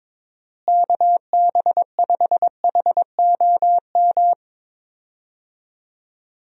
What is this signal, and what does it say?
Morse code, heard as a keyed tone, about 22 wpm, 700 Hz. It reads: KB5HOM
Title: USA Call Signs - 6 Characters - Repeated 3x - 22wpm